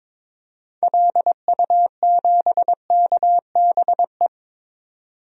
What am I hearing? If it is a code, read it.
Morse code, heard as a keyed tone, about 21 wpm, 700 Hz. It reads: LU7KBE
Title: DXCC-100 Call Signs - 6 Characters - 22wpm